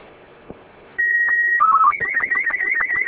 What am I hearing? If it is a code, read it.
SSTV (slow-scan television), Robot 36.